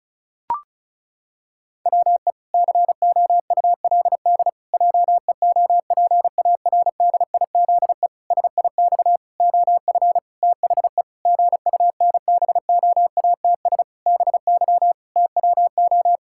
Morse code, heard as a keyed tone, about 34 wpm, 700 Hz. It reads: WE COULD JEOPARDIZE SIX OF THE GUNBOATS BY TWO
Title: Pangram Challenge v2 - 35wpm